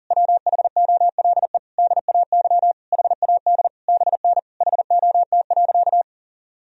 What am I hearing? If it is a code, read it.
Morse code, heard as a keyed tone, about 40 wpm, 700 Hz. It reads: WHOLE DAY HAD BN HOT.
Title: Sentences from Top 500 Words and Abbreviations 40wpm